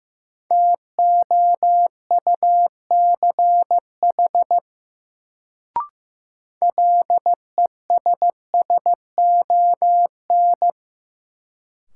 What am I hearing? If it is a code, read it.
Morse code, heard as a keyed tone, about 15 wpm, 700 Hz. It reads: TOUCH LESSON
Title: Sentences from Top 1200 Words - New Word Drill Alt - 15wpm